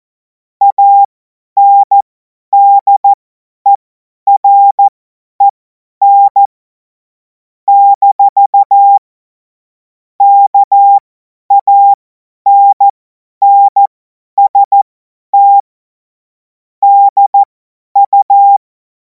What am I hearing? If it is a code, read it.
Morse code, heard as a keyed tone, about 14 wpm, spaced as for 10 wpm, 800 Hz. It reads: ANDEREN - KANNST DU
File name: Morsekurs / Witze / Witz 31. März 2026 50 F70